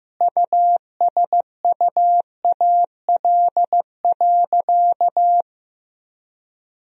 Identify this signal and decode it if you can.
Morse code, heard as a keyed tone, about 15 wpm, 700 Hz. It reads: USUAL.
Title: Sentences from Top 1000 Words - Review Part 2 - 15wpm